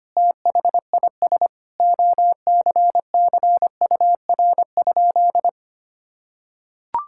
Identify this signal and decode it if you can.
Morse code, heard as a keyed tone, about 25 wpm, 700 Hz. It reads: THIS OCCUR?
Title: Sentences from Top 1000 Words - Review Part 6 - 25wpm